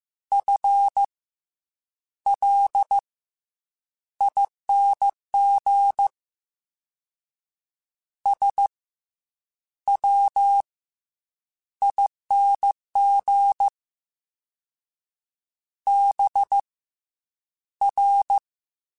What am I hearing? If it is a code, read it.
Morse code, heard as a keyed tone, about 15 wpm, spaced as for 8 wpm, 800 Hz. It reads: FLING SWING BR
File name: Misc / Misc-mp3s / Jumps @13